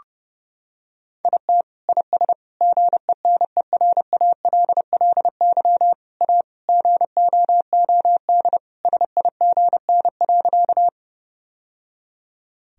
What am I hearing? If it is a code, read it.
Morse code, heard as a keyed tone, about 30 wpm, 700 Hz. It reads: IT IS GENERALLY A GOOD SIGN.
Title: Sentences from Top 1000 Words - Review Part 4 - 30wpm